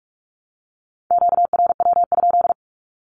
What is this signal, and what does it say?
Morse code, heard as a keyed tone, about 44 wpm, 700 Hz. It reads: QRW?
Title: Q-Codes Repeated 3x 45wpm